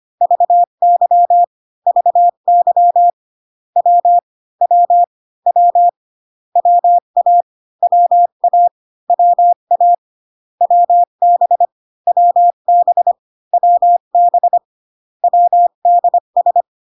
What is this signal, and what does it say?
Morse code, heard as a keyed tone, about 25 wpm, 700 Hz. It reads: VY VY W W W WA WA WA WB WB WB WDS